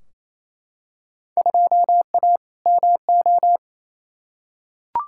Morse code, heard as a keyed tone, about 28 wpm, 700 Hz. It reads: 2A MO
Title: Field Day Exchanges Encore - 28wpm